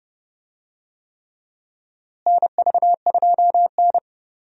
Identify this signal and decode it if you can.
Morse code, heard as a keyed tone, about 30 wpm, 700 Hz. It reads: NV2N
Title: USA Call Signs - 4 Characters - Repeated 3x - 30wpm